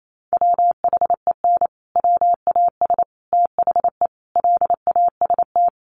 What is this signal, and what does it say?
Morse code, heard as a keyed tone, about 28 wpm, 700 Hz. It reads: WHEN WAS THE LAST
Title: Sets of 5 Words Encore 28wpm